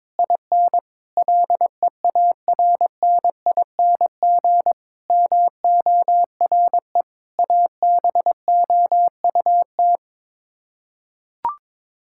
Morse code, heard as a keyed tone, about 22 wpm, 700 Hz. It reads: IN LEARNING MORE ABOUT
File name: practice / 22wpm / Sets of 8 Words - 22wpm